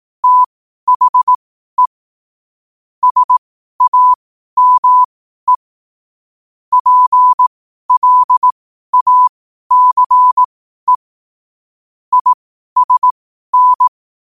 Morse code, heard as a keyed tone, about 18 wpm, spaced as for 12 wpm, 1000 Hz. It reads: THE SAME PLACE ISN